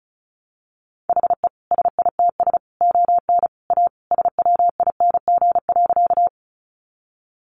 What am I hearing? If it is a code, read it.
Morse code, heard as a keyed tone, about 34 wpm, 700 Hz. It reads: HE SITS ON A SWING.
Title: Additional Easy Phrases 35wpm